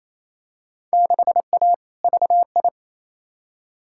Morse code, heard as a keyed tone, about 28 wpm, 700 Hz. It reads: 6A VI
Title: Field Day Exchanges Encore - 28wpm